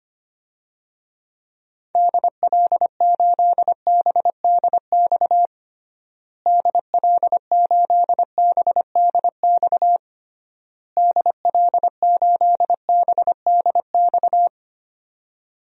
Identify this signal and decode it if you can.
Morse code, heard as a keyed tone, about 25 wpm, 700 Hz. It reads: DL8BDX DL8BDX DL8BDX
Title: DXCC-100 Call Signs - 6 Characters - Repeated 3x - 25wpm